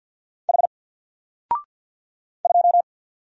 Morse code, heard as a keyed tone, about 50 wpm, 700 Hz. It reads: H 2
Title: Single Character (Letter:Number:Punctuation:Prosign) Alt 50wpm